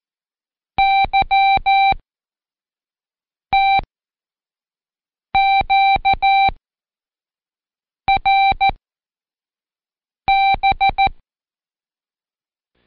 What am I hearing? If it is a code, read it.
Morse code, heard as a keyed tone, about 14 wpm, 780 Hz. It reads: Y T Q R B
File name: Code Quick / AudioDownld1 / Seg74-Day8-Pract 1234